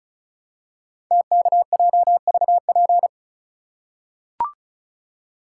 Morse code, heard as a keyed tone, about 36 wpm, 690 Hz. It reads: TKJVP
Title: Code Groups - 5 Letters - 35wpm